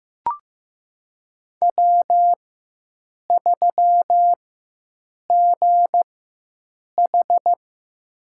Morse code, heard as a keyed tone, about 15 wpm, spaced as for 7 wpm, 700 Hz. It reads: W3GH
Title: USA Call Signs - 4 Characters - 4x Character Spacing - 15wpm